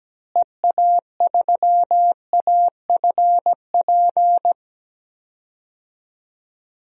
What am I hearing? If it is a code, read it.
Morse code, heard as a keyed tone, about 17 wpm, 700 Hz. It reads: EA3AFP
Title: DXCC-100 Call Signs - 6 Characters - Repeated 2x - 17wpm